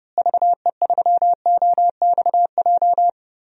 Morse code, heard as a keyed tone, about 29 wpm, 700 Hz. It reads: VE3OXJ
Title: DXCC-100 Call Signs - 6 Characters - Repeated 3x - 30wpm